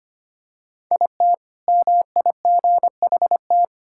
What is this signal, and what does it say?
Morse code, heard as a keyed tone, about 25 wpm, 700 Hz. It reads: IT MIGHT
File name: practice / 25wpm / Sentences with 2-Word Phrases - 25wpm